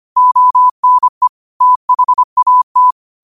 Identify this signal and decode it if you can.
Morse code, heard as a keyed tone, about 25 wpm, 1000 Hz. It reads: ONE THAT
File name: qotdfiles / 25wpm